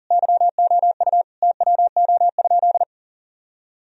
Morse code, heard as a keyed tone, about 40 wpm, 700 Hz. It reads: YOU TWO?
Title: Sentences from Top 1000 Words - Review Part 1 - 40wpm